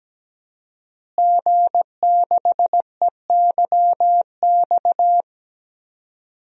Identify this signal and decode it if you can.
Morse code, heard as a keyed tone, about 17 wpm, 700 Hz. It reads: G6EYX
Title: DXCC-100 Call Signs - 5 Characters - Repeated 3x - 17wpm